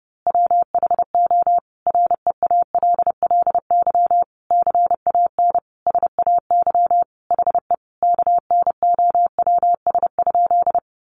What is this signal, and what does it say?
Morse code, heard as a keyed tone, about 30 wpm, 700 Hz. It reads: WHO REALLY CAN SAY HE KNOWS?
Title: A Sound Of Thunder 30wpm